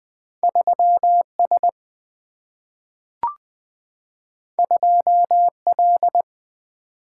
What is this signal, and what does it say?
Morse code, heard as a keyed tone, about 20 wpm, 700 Hz. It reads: 3 S 2 L
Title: Single-Letter Single-Number 20wpm